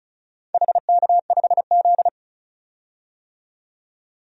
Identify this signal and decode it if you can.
Morse code, heard as a keyed tone, about 36 wpm, 700 Hz. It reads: HK5Z
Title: DXCC-100 Call Signs - 4 Characters - 35wpm